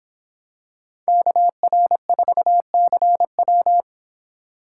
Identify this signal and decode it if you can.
Morse code, heard as a keyed tone, about 25 wpm, 700 Hz. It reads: KR4CW